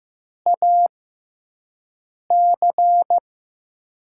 Morse code, heard as a keyed tone, about 15 wpm, spaced as for 5 wpm, 700 Hz. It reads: AC
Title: USA Call Signs - 5 Characters - 6x Character Spacing - 15wpm